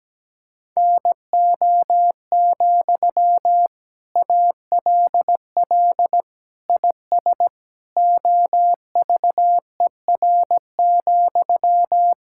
Morse code, heard as a keyed tone, about 17 wpm, 700 Hz. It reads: NO, ALL IS OVER,
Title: Sentences from Top 1600 Words 17wpm